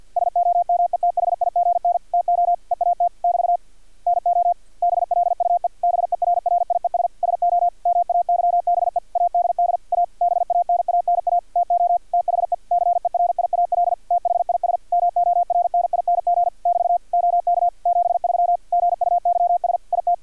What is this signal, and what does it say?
Morse code, heard as a keyed tone, about 50 wpm, 700 Hz, with radio noise. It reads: SOMETHIGN TO EAT <BT> NO BLUE BERRIES SO MAYBE ADD A BANANA TO THE CERIAL THIS MORNING <BT> OK 73 GUYS IT
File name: archived / CRSnet